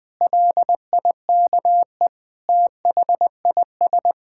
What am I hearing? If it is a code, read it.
Morse code, heard as a keyed tone, about 21 wpm, 700 Hz. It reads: LIKE THIS
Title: Sentences from Top 1000 Words - Review Part 5 - 20wpm